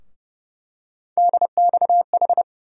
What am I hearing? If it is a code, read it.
Morse code, heard as a keyed tone, about 30 wpm, 700 Hz. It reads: DXH
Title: USA Call Sign Postfixes - 3 Letters 30wpm